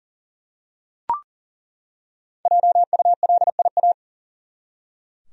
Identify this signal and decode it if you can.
Morse code, heard as a keyed tone, about 40 wpm, 700 Hz. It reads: JULIA
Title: USA Common Names - Max 5 Letters - 40wpm